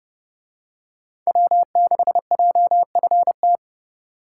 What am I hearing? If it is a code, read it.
Morse code, heard as a keyed tone, about 30 wpm, 700 Hz. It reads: W6JFT